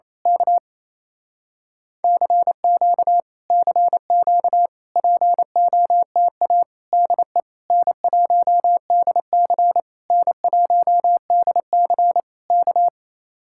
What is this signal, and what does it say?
Morse code, heard as a keyed tone, about 26 wpm, 700 Hz. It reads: K CQ CQ POTA DE N1DC N1DC K
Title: Speed-Racing Common QSO Phrases Encore - 28wpm